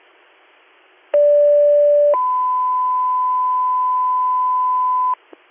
Two-tone page, heard as a one-second tone, then a three-second tone, about 580 then 1010 Hz, with noise. It is heard from 1.1 to 5.1 s.